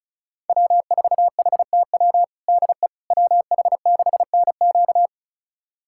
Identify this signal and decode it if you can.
Morse code, heard as a keyed tone, about 34 wpm, 700 Hz. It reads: W4HTW DE WH6NQ
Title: USA Callsign DE Callsign - 5 Characters - 35wpm